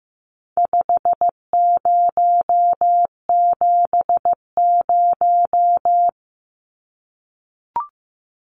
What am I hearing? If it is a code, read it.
Morse code, heard as a keyed tone, about 15 wpm, 700 Hz. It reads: 5070
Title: Numbers - 4-Digit 15wpm